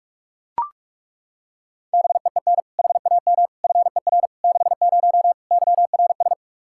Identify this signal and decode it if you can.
Morse code, heard as a keyed tone, about 41 wpm, 700 Hz. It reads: BEEN HAM FER 60 YRS
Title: Practice QSO - Exchange 3 - More Info - 45wpm